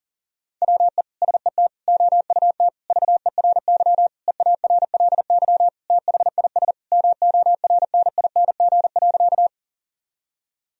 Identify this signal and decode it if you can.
Morse code, heard as a keyed tone, about 40 wpm, 700 Hz. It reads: WE SET OUT VERY EARLY THIS MORNING.
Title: Sentences from Top 1000 Words - Review Part 2 - 40wpm